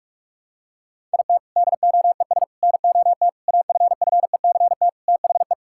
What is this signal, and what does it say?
Morse code, heard as a keyed tone, about 41 wpm, 700 Hz. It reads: IT DOES NOT AFFECT THE
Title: Sentences from Top 1000 Words - Review Part 6 - 45wpm